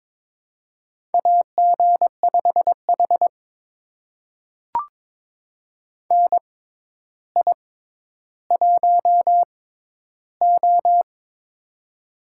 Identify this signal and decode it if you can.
Morse code, heard as a keyed tone, about 22 wpm, spaced as for 11 wpm, 700 Hz. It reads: AG5H NI1O
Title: USA Call Signs - 4 Characters - 6x Character Spacing - 22wpm